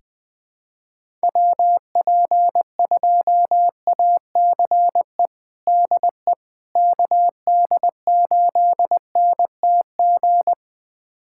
Morse code, heard as a keyed tone, about 21 wpm, 700 Hz. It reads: WP2ACE DE KD8NTG
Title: USA Callsign DE Callsign - 6 Characters - 20wpm